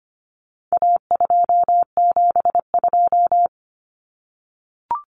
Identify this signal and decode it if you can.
Morse code, heard as a keyed tone, about 25 wpm, 700 Hz. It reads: A272
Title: Common Cut Numbers - 4-Digit 25wpm